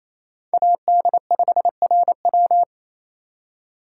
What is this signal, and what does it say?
Morse code, heard as a keyed tone, about 27 wpm, 700 Hz. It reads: AD5RW
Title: USA Call Signs - 3x Character Spacing - 28wpm